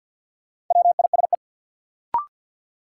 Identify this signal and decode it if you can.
Morse code, heard as a keyed tone, about 53 wpm, 700 Hz. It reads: WISE
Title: Sentences from Top 1600 Words - New Word Drill - 50wpm